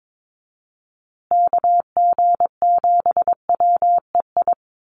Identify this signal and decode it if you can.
Morse code, heard as a keyed tone, about 22 wpm, 700 Hz. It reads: KG7WEI
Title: USA Call Signs Repeated 2x 22wpm